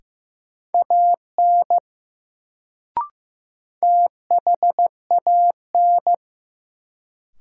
Morse code, heard as a keyed tone, about 15 wpm, 700 Hz. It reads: AN THAN
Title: Sets of 2 Words - New Word Drill - 15wpm